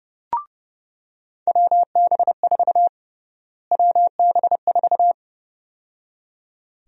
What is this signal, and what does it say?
Morse code, heard as a keyed tone, about 30 wpm, 700 Hz. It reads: WB4 WB4
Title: USA Call Sign Prefixes - Repeated 2x - 30wpm